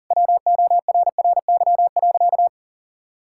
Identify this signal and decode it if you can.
Morse code, heard as a keyed tone, about 40 wpm, 700 Hz. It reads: WORRY.
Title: Pangram Challenge v2 - 40wpm